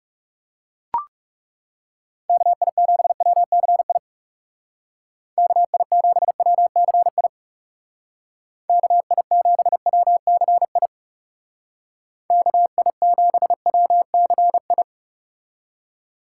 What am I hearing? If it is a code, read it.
Morse code, heard as a keyed tone, about 40 wpm, 700 Hz. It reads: KI7WCI KI7WCI KI7WCI KI7WCI